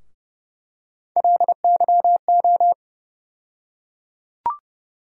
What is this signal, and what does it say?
Morse code, heard as a keyed tone, about 30 wpm, 700 Hz. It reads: LYO
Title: USA Call Sign Postfixes - 3 Letters 30wpm